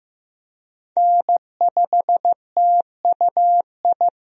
Morse code, heard as a keyed tone, about 15 wpm, 700 Hz. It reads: N5TUI
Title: USA Call Signs - 5 Characters - 15wpm